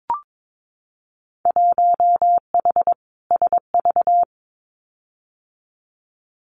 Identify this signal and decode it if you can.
Morse code, heard as a keyed tone, about 22 wpm, 700 Hz. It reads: 1H SV
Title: Winter Field Day Exchanges 22wpm